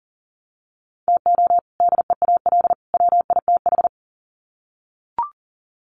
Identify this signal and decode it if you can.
Morse code, heard as a keyed tone, about 38 wpm, 700 Hz. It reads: TO DEAL WITH
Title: Sets of 3 Words 40wpm